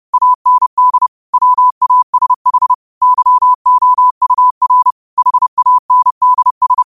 Morse code, heard as a keyed tone, about 30 wpm, 1000 Hz. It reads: AND WASH YOUR HANDS